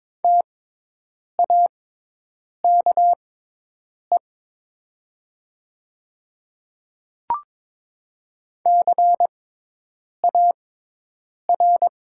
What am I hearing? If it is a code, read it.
Morse code, heard as a keyed tone, about 22 wpm, spaced as for 8 wpm, 700 Hz. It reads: TAKE CAR
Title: 6x Character Spacing - Top 300 Words - 22wpm